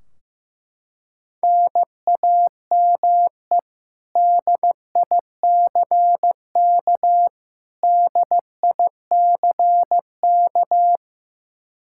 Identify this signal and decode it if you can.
Morse code, heard as a keyed tone, about 15 wpm, 700 Hz. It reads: NAME DICK DICK
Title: Practice QSO - Exchange 1 - Beginning - 15wpm